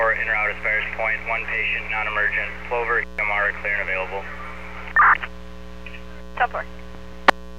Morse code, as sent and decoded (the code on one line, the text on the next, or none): none